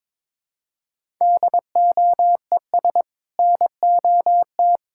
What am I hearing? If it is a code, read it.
Morse code, heard as a keyed tone, about 22 wpm, 700 Hz. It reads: DOES NOT
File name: practice / 22wpm / Build A Phrase - 5-Word Phrases - 22wpm